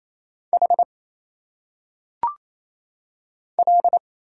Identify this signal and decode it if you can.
Morse code, heard as a keyed tone, about 27 wpm, 700 Hz. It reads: H L
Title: Single Letters - Phonetic Pronunciation 28wpm